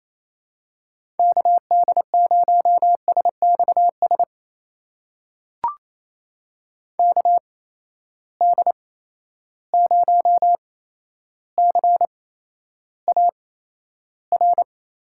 Morse code, heard as a keyed tone, about 28 wpm, spaced as for 12 wpm, 700 Hz. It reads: KD0SXS KD0CAR
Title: USA Call Signs - 6 Characters - 8x Character Spacing - 28wpm